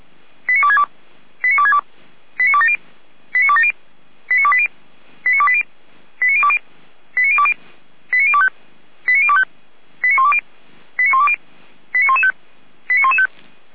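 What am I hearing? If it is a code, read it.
Five-tone selective call (ZVEI), heard as quick runs of five tones, tones of about 70 ms. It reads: 89262 89262 89270 89270 89280 89280 89020 89020 89025 89025 89120 89120 89115 89115
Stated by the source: ZVEI five-tone selective call 89358, received via SDR.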